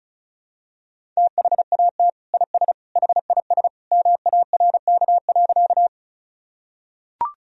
Morse code, heard as a keyed tone, about 36 wpm, 700 Hz. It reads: THAT IS HIS MARK.
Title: Sentences from Top 1000 Words - Review Part 1 - 35wpm